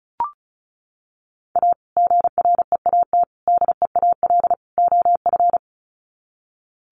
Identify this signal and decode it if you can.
Morse code, heard as a keyed tone, about 34 wpm, 700 Hz. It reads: A GREAT DEAL OF
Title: Sets of 4 Words 35wpm